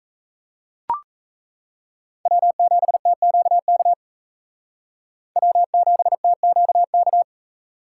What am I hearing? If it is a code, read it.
Morse code, heard as a keyed tone, about 34 wpm, 700 Hz. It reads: W7TQK W7TQK
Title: Speed Racing - USA 1x3 Call Signs 28wpm